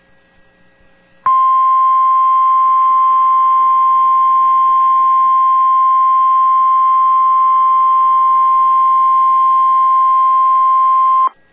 Emergency Alert System (SAME) tones (the 1050 Hz attention tone). It is heard from 1.2 to 11.3 s.